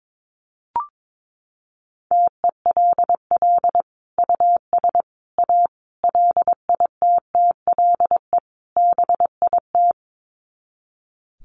Morse code, heard as a keyed tone, about 22 wpm, 700 Hz. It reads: TELL US A LITTLE BIT
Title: Sets of 5 Words Encore2 22wpm